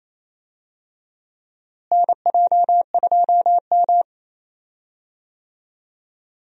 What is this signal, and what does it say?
Morse code, heard as a keyed tone, about 27 wpm, 700 Hz. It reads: NJ2M
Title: USA Call Signs - 4 Characters - Repeated 2x - 28wpm